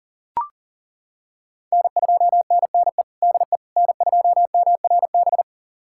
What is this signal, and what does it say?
Morse code, heard as a keyed tone, about 37 wpm, 700 Hz. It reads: N2NNE DE N2MRB
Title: USA Callsign DE Callsign - 5 Characters - 40wpm